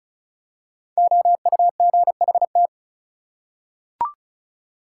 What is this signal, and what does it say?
Morse code, heard as a keyed tone, about 36 wpm, 700 Hz. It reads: OUGHT
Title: Five Letter Words - 35wpm